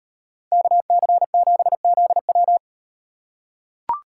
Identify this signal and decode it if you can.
Morse code, heard as a keyed tone, about 37 wpm, 700 Hz. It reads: KC7ZW